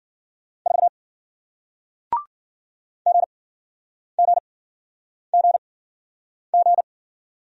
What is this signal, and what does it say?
Morse code, heard as a keyed tone, about 54 wpm, spaced as for 9 wpm, 700 Hz. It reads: 4 GGGG